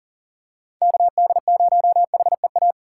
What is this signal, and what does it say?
Morse code, heard as a keyed tone, about 40 wpm, 700 Hz. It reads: KD0HEA
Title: USA Call Signs - 6 Characters - 3x Character Spacing - 40wpm